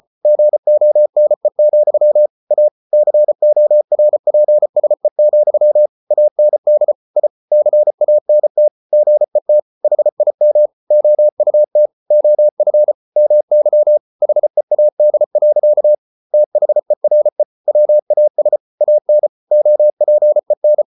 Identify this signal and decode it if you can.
Morse code, heard as a keyed tone, about 34 wpm, 600 Hz. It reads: GONE, A CORPSE, AND I CANT GET HIM OUT OF MY HEAD. THERE WAS AN OPEN